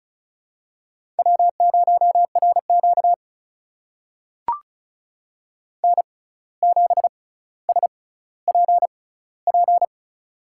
Morse code, heard as a keyed tone, about 36 wpm, spaced as for 17 wpm, 700 Hz. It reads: W0RQ N7SPP
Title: USA Call Signs - 6x Character Spacing - 35wpm